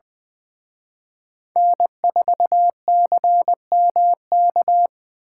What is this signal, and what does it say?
Morse code, heard as a keyed tone, about 23 wpm, 700 Hz. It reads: N4CMK
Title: Speed Racing - USA Call Signs 20wpm